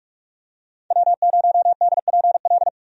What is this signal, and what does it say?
Morse code, heard as a keyed tone, about 47 wpm, 700 Hz. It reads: W0DPL
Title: USA Call Signs - 5 Characters - 45wpm